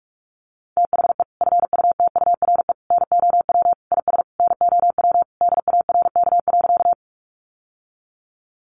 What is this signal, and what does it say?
Morse code, heard as a keyed tone, about 44 wpm, 700 Hz. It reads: THE FUTURE NOW IS NOW DARK.
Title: Sentences from Top 700 Words 45wpm